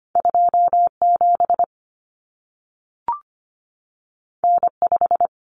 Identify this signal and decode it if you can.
Morse code, heard as a keyed tone, about 25 wpm, 700 Hz. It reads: 27 N5
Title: Common Cut Numbers - 2-Digit 25wpm